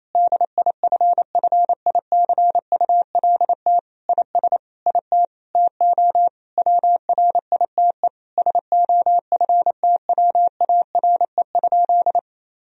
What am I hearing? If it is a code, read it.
Morse code, heard as a keyed tone, about 28 wpm, 700 Hz. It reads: DIFFICULT IS IT TO WRITE SOFTWARE?
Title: Sentences from Top 1000 Words - Review Part 3 - 28wpm